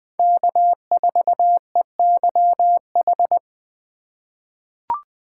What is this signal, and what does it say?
Morse code, heard as a keyed tone, about 20 wpm, 700 Hz. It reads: K4EYH
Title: USA Call Signs - 5 Characters - 4x Character Spacing - 20wpm